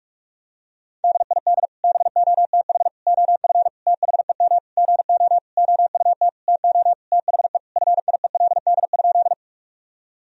Morse code, heard as a keyed tone, about 46 wpm, 700 Hz. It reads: DID BOTH OF THEM GO OUT TO THE FIELD?
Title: Sentences from Top 1000 Words - Review Part 2 - 45wpm